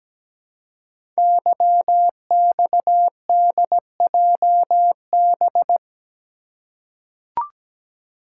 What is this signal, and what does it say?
Morse code, heard as a keyed tone, about 17 wpm, 700 Hz. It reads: YXDJB
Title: Code Groups - 5 Letters - 17wpm